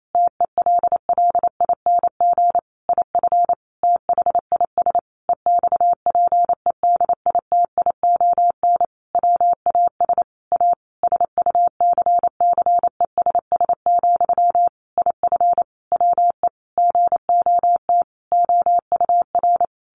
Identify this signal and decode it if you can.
Morse code, heard as a keyed tone, about 28 wpm, 700 Hz. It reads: TELLING IF THIS EXPEDITION WAS A SUCCESS, IF WE GOT OUR
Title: A Sound Of Thunder 28wpm